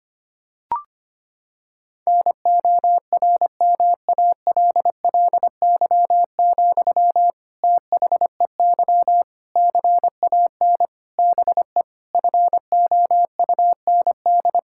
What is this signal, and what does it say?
Morse code, heard as a keyed tone, about 25 wpm, 700 Hz. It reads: NORMALLY, THEY CAN BE FOUND
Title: Sentences from Top 1000 Words - Review Part 6 - 25wpm